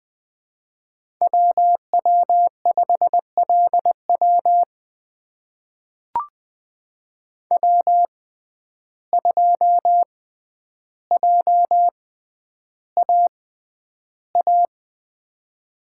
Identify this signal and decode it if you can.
Morse code, heard as a keyed tone, about 20 wpm, spaced as for 10 wpm, 700 Hz. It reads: WW5LW W2JAA
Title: USA Call Signs - 5 Characters - 6x Character Spacing - 20wpm